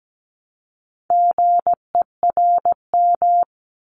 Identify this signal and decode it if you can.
Morse code, heard as a keyed tone, about 17 wpm, 700 Hz. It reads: GERM
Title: Multi-letter Phonemes Repeated 3x 17wpm